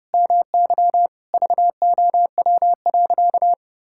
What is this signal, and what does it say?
Morse code, heard as a keyed tone, about 30 wpm, 700 Hz. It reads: MY VOW.
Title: Pangram Challenge v2 - 30wpm